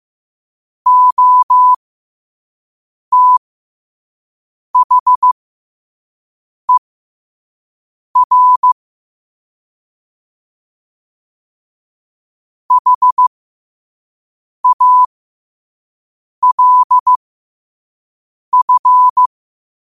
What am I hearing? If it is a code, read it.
Morse code, heard as a keyed tone, about 15 wpm, spaced as for 5 wpm, 1000 Hz. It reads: OTHER HALF
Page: Quotes for Thu, 14 Aug 2025 in Morse Code at 5 words per minute.